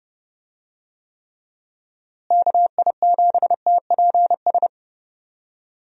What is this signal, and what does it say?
Morse code, heard as a keyed tone, about 30 wpm, 700 Hz. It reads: KI7TPS
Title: USA Call Signs - 6 Characters - Repeated 2x - 30wpm